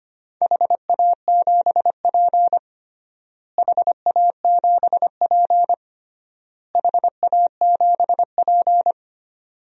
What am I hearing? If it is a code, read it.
Morse code, heard as a keyed tone, about 25 wpm, 700 Hz. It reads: HA7P HA7P HA7P
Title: DXCC-100 Call Signs - 4 Characters - Repeated 3x - 25wpm